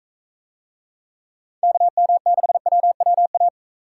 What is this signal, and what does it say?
Morse code, heard as a keyed tone, about 42 wpm, 700 Hz. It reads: KM6WWA